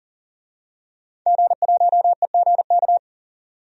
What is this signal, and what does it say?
Morse code, heard as a keyed tone, about 38 wpm, 700 Hz. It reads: G1EGK
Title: DXCC-100 Call Signs - 5 Characters - 40wpm